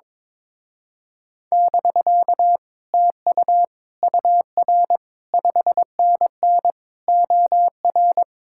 Morse code, heard as a keyed tone, about 21 wpm, 700 Hz. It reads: <BK> TU UR 5NN OR
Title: Speed-Racing Common QSO Phrases Encore - 22wpm